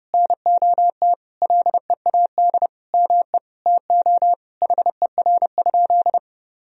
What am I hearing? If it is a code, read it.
Morse code, heard as a keyed tone, about 30 wpm, 700 Hz. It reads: NOT LEAD ME TO HER?
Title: Sentences from Top 1000 Words - Review Part 2 - 30wpm